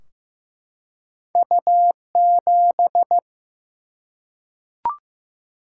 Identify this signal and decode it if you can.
Morse code, heard as a keyed tone, about 15 wpm, 700 Hz. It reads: U7
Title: Single-Letter Single-Number 15wpm